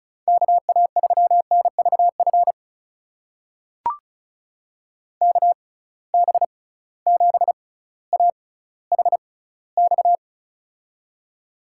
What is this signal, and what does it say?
Morse code, heard as a keyed tone, about 34 wpm, spaced as for 18 wpm, 700 Hz. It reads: KA3NVF KB7AHX
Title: USA Call Signs - 6 Characters - 6x Character Spacing - 35wpm